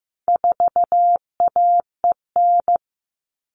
Morse code, heard as a keyed tone, about 15 wpm, 700 Hz. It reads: 4AEN
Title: All Cut Numbers - 4-Digit 15wpm